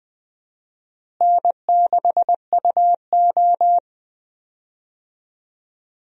Morse code, heard as a keyed tone, about 20 wpm, 700 Hz. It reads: N6UO
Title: USA Call Signs - 4 Characters - Repeated 3x - 20wpm